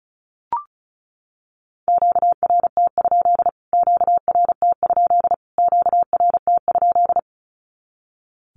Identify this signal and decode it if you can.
Morse code, heard as a keyed tone, about 34 wpm, 700 Hz. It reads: QRT? QRT? QRT?
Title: Q-Codes Repeated 3x 35wpm